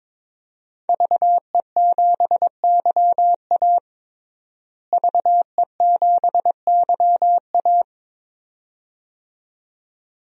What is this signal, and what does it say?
Morse code, heard as a keyed tone, about 22 wpm, 700 Hz. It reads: VE7YA VE7YA
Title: DXCC-100 Call Signs - 5 Characters - Repeated 2x - 22wpm